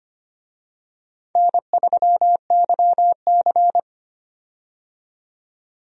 Morse code, heard as a keyed tone, about 25 wpm, 700 Hz. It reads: N3YC
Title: USA Call Signs - 4 Characters - 25wpm